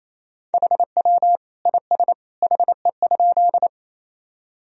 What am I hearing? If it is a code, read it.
Morse code, heard as a keyed tone, about 28 wpm, 700 Hz. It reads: HW IS HE?
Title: Sentences from Top 100 Words and Abbreviations 28wpm